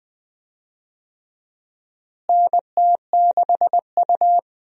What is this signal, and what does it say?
Morse code, heard as a keyed tone, about 20 wpm, 700 Hz. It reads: NT6U
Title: USA Call Signs - 4 Characters - Repeated 2x - 20wpm